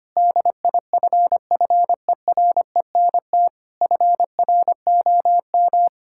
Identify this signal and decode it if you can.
Morse code, heard as a keyed tone, about 25 wpm, 700 Hz. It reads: DIFFERENT FROM
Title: Sentences from Top 1000 Words - Review Part 5 - 25wpm